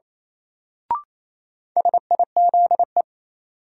Morse code, heard as a keyed tone, about 28 wpm, 700 Hz. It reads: SIZE
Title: Sets of 3 Words - New Word Drill - 28wpm